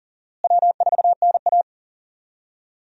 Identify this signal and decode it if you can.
Morse code, heard as a keyed tone, about 40 wpm, 700 Hz. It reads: W4NA
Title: USA Call Signs - 4 Characters - Repeated 2x - 40wpm